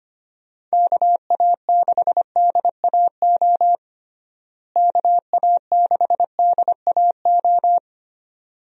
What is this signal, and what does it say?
Morse code, heard as a keyed tone, about 25 wpm, 700 Hz. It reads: KA6DAO KA6DAO
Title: USA Call Signs - 6 Characters - Repeated 2x - 25wpm